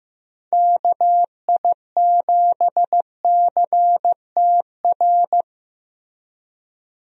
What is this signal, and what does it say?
Morse code, heard as a keyed tone, about 15 wpm, 700 Hz. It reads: KI7CTR
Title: USA Call Signs - 6 Characters - Repeated 2x - 15wpm